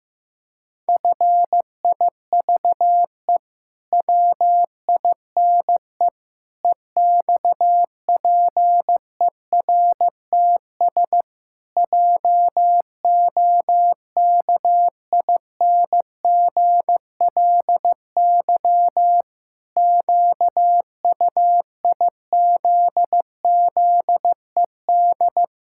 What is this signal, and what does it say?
Morse code, heard as a keyed tone, about 15 wpm, 700 Hz. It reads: FIVE WINE EXPERTS JOKINGLY QUIZZED
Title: Pangram Challenge v2 - 15wpm